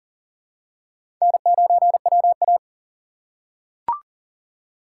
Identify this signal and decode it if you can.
Morse code, heard as a keyed tone, about 38 wpm, 700 Hz. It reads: N9WA
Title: USA Call Signs - 4x Character Spacing - 40wpm